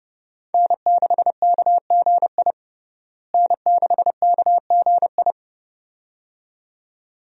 Morse code, heard as a keyed tone, about 30 wpm, 700 Hz. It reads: N6KGI N6KGI
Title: USA Call Signs - 5 Characters - Repeated 2x - 30wpm